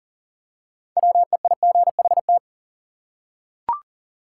Morse code, heard as a keyed tone, about 40 wpm, 700 Hz. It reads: WEIGHT
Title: Six Letter Words - 40wpm